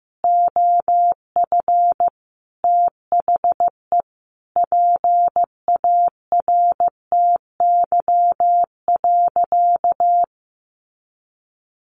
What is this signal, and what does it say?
Morse code, heard as a keyed tone, about 15 wpm, 700 Hz. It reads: OF THE PARTY.
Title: Sentences from Top 1400 Words 15wpm